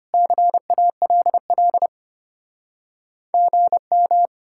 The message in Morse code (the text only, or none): CALL GM